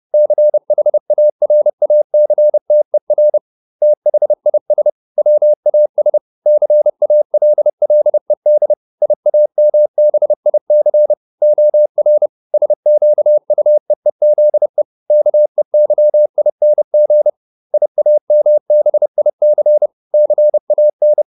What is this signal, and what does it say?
Morse code, heard as a keyed tone, about 30 wpm, 600 Hz. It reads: CHARACTER THIS WAS CALLED IAMBIC OR SQUEEZE KEYING IAMBIC CAN